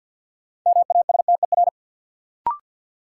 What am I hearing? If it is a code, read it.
Morse code, heard as a keyed tone, about 51 wpm, 700 Hz. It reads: MASTER
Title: Sentences from Top 1300 Words - New Word Drill - 50wpm